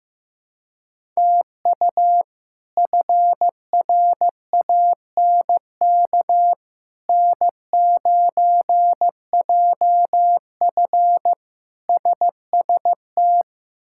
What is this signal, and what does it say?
Morse code, heard as a keyed tone, about 15 wpm, 700 Hz. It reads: TU FRANK N9JF SST
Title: K1USN SST QSO Practice - 15wpm